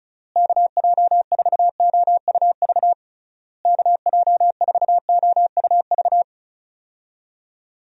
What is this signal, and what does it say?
Morse code, heard as a keyed tone, about 34 wpm, 700 Hz. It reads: KJ4OUV KJ4OUV
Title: USA Call Signs - 6 Characters - Repeated 3x - 35wpm